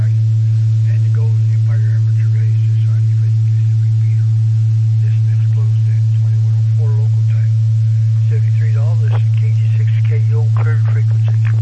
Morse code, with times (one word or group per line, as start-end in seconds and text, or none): none